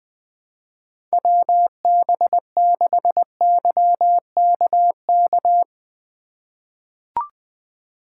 Morse code, heard as a keyed tone, about 21 wpm, 700 Hz. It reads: WB6YKK
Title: USA Call Signs - 6 Characters - 3x Character Spacing - 20wpm